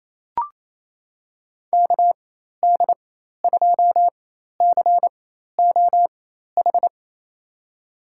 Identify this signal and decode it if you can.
Morse code, heard as a keyed tone, about 28 wpm, spaced as for 13 wpm, 700 Hz. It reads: KD2COH
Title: USA Call Signs - 6 Characters - 4x Character Spacing - 28wpm